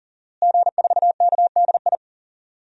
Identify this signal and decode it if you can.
Morse code, heard as a keyed tone, about 40 wpm, 700 Hz. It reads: G4KDI
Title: DXCC-100 Call Signs - 5 Characters - Repeated 3x - 40wpm